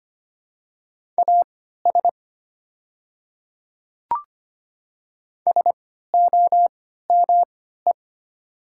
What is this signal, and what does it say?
Morse code, heard as a keyed tone, about 25 wpm, spaced as for 14 wpm, 700 Hz. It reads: AS SOME
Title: 3x Character Spacing - Top 100 Words - 25wpm